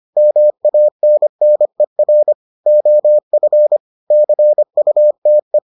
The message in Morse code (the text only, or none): MANNER OF CUTE